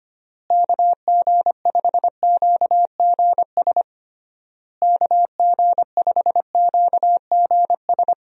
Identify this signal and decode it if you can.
Morse code, heard as a keyed tone, about 25 wpm, 700 Hz. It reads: KG5QGS KG5QGS
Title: USA Call Signs - 6 Characters - Repeated 3x - 25wpm